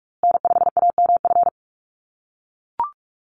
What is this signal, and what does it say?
Morse code, heard as a keyed tone, about 44 wpm, 700 Hz. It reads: N5AMF
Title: USA Call Signs Repeated 3x 45wpm